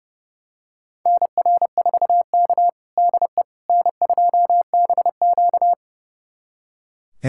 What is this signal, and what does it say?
Morse code, heard as a keyed tone, about 30 wpm, 700 Hz. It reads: NR4K DE N2BQ
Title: USA Callsign DE Callsign - 4 Characters - 30wpm